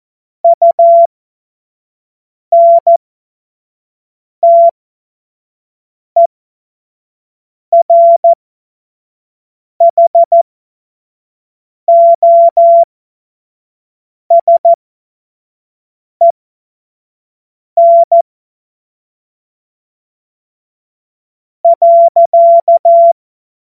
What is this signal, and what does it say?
Morse code, heard as a keyed tone, about 14 wpm, spaced as for 5 wpm, 680 Hz. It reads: UNTERHOSEN.
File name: Morsekurs / Witze / Witz 01. Februar 2022 25 F70